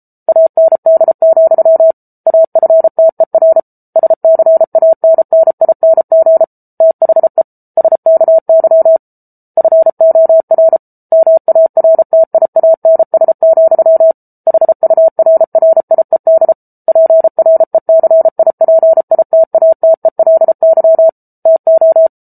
Morse code, heard as a keyed tone, about 34 wpm, 650 Hz. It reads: AND, AFTER SCANNING THE SKY FOR MARTIANS, HURRIED PRECIPITATELY TO